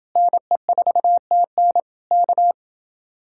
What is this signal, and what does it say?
Morse code, heard as a keyed tone, about 29 wpm, 700 Hz. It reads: NE4TN K